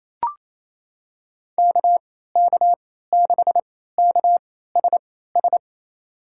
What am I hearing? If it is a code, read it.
Morse code, heard as a keyed tone, about 28 wpm, spaced as for 16 wpm, 700 Hz. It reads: KK6KSS
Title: USA Call Signs - 6 Characters - 3x Character Spacing - 28wpm